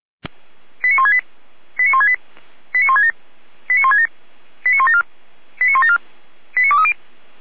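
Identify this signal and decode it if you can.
Five-tone selective call (ZVEI), heard as quick runs of five tones, tones of about 70 ms. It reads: 89168 89168 89167 89167 89174 89174 89230
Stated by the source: ZVEI five-tone selective call received on an SDR.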